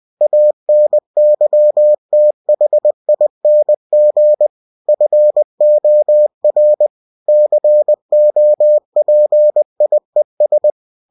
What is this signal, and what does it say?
Morse code, heard as a keyed tone, about 20 wpm, 600 Hz. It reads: ANYTHING FOR COPIES
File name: cw-books / winnie-the-pooh / winnie-20wpm-page-0112